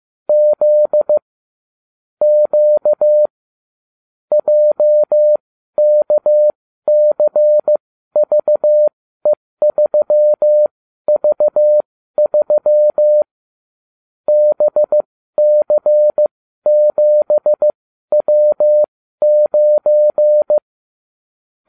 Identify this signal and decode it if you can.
Morse code, heard as a keyed tone, about 15 wpm, spaced as for 12 wpm, 600 Hz. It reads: Z Q JKCVE3V3 BC7W9